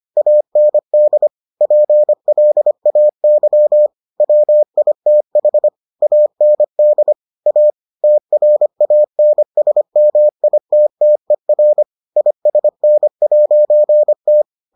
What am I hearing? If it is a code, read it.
Morse code, heard as a keyed tone, about 25 wpm, 600 Hz. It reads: AND PLAY WITH AND A TRANSMITTER ISN'T